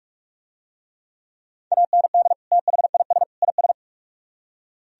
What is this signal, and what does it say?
Morse code, heard as a keyed tone, about 40 wpm, 700 Hz. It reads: AND THIS IS